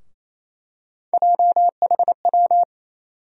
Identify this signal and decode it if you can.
Morse code, heard as a keyed tone, about 28 wpm, 700 Hz. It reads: JHW
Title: USA Call Sign Postfixes - 3 Letters 28wpm